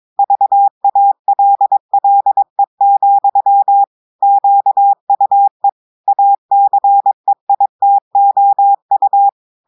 Morse code, heard as a keyed tone, about 22 wpm, 800 Hz. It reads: VALLE, QUE ACEITOU
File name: audios / portugais / portugais texte en clair 1 vitesse 22 mots minutes